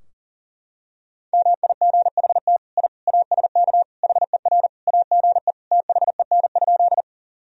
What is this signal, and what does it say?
Morse code, heard as a keyed tone, about 40 wpm, 700 Hz. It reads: MIGHT I ASK HER AGE THEN?
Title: Sentences from Top 1000 Words - Review Part 3 - 40wpm